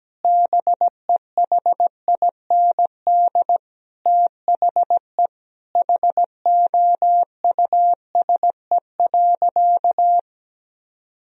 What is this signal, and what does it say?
Morse code, heard as a keyed tone, about 17 wpm, 700 Hz. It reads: BEHIND THE HOUSE.
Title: Sentences from Top 1000 Words - Review Part 4 - 17wpm